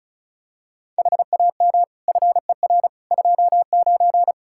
Morse code, heard as a keyed tone, about 36 wpm, 700 Hz. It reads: HAM FER 29
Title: Practice QSO - Exchange 3 - More Info - 35wpm